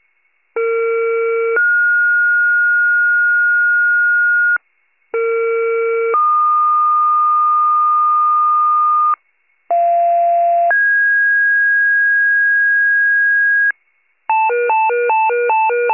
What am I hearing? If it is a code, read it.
Two-tone page, heard as a one-second tone, then a three-second tone, about 460 then 1430 Hz, about 460 then 1220 Hz, about 690 then 1650 Hz, with noise. It is heard from 0.6 to 13.7 s.
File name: pager / 21 RESCUE / wcalert